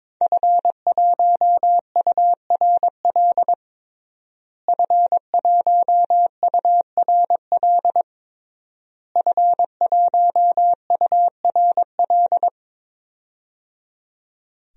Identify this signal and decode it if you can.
Morse code, heard as a keyed tone, about 22 wpm, 700 Hz. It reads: F1URL F1URL F1URL
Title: DXCC-100 Call Signs - 5 Characters - Repeated 3x - 22wpm